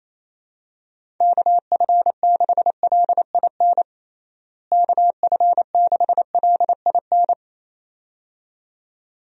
Morse code, heard as a keyed tone, about 28 wpm, 700 Hz. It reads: KF6LIN KF6LIN
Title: USA Call Signs - 6 Characters - Repeated 2x - 28wpm